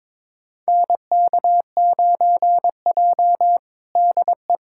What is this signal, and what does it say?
Morse code, heard as a keyed tone, about 21 wpm, 700 Hz. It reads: NK9J DE
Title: USA Callsign DE Callsign - 4 Characters - 22wpm